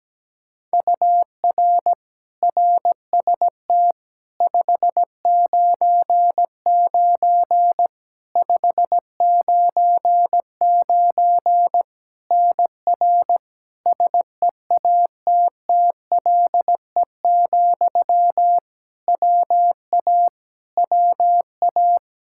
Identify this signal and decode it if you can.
Morse code, heard as a keyed tone, about 17 wpm, 700 Hz. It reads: UR RST 599 599 NR SEATTLE, WA WA
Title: Practice QSO - Exchange 1 - Beginning - 17wpm